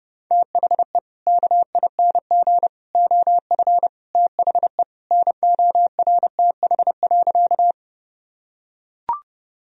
Morse code, heard as a keyed tone, about 30 wpm, 700 Hz. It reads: THE KING OF THE NORTH.
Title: Sentences from Top 1000 Words - Review Part 1 - 30wpm